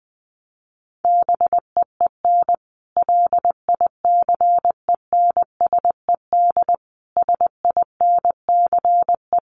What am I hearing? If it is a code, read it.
Morse code, heard as a keyed tone, about 19 wpm, 700 Hz. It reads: BEEN LICENSED SINCE
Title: Common QSO Phrases 20wpm